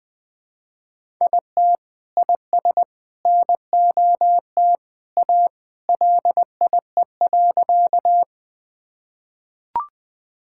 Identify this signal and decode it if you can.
Morse code, heard as a keyed tone, about 20 wpm, 700 Hz. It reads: IT IS NOT A LIE.
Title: Sentences from Top 1600 Words 20wpm